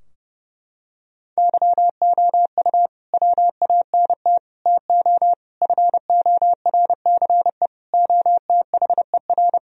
Morse code, heard as a keyed tone, about 32 wpm, 700 Hz. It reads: YOU WANT TO FORCE OTHER
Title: Sentences from Top 1000 Words - Review Part 3 - 30wpm